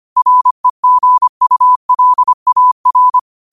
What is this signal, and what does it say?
Morse code, heard as a keyed tone, about 25 wpm, 1000 Hz. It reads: REGULAR